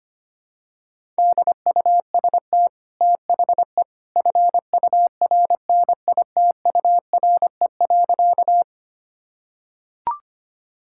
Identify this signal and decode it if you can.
Morse code, heard as a keyed tone, about 25 wpm, 700 Hz. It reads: DUST THE FURNITURE.
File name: Easy Sentences with Hints - 25wpm